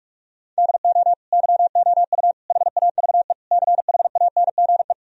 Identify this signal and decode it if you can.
Morse code, heard as a keyed tone, about 41 wpm, 700 Hz. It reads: DO YOU HAVE CHANGE
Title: Sentences from Top 1000 Words - Review Part 2 - 45wpm